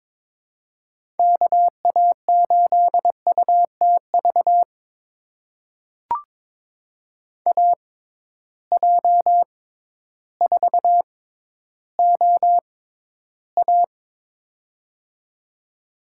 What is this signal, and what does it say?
Morse code, heard as a keyed tone, about 21 wpm, spaced as for 12 wpm, 700 Hz. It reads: KA8UTV AJ4OA
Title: USA Call Signs - 6x Character Spacing - 22wpm